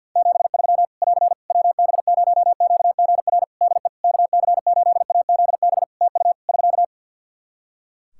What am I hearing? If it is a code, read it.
Morse code, heard as a keyed tone, about 51 wpm, 700 Hz. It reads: 73 <AR> WB0QGR DE KC8AZB TU <SK>
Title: Practice QSO - Exchange 4 - Ending - 50wpm